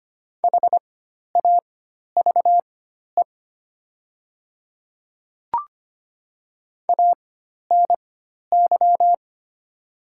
Morse code, heard as a keyed tone, about 25 wpm, spaced as for 12 wpm, 700 Hz. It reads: HAVE ANY
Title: 4x Character Spacing - Top 500 Words - 25wpm